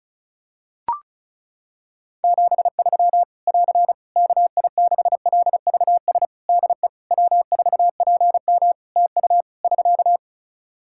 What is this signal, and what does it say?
Morse code, heard as a keyed tone, about 36 wpm, 700 Hz. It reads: 73 <AR> KI6LVS DE W4PM TU <SK>
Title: Practice QSO - Exchange 4 - Ending - 35wpm